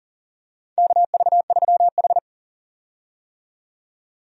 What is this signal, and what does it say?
Morse code, heard as a keyed tone, about 40 wpm, 700 Hz. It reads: KV3H
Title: USA Call Signs - 4 Characters - 40wpm